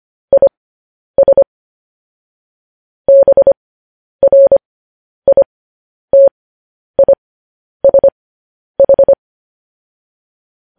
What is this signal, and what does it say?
Morse code, heard as a keyed tone, about 25 wpm, spaced as for 10 wpm, 550 Hz. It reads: IS BRITISH
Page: news headlines in morse code at 10 wpm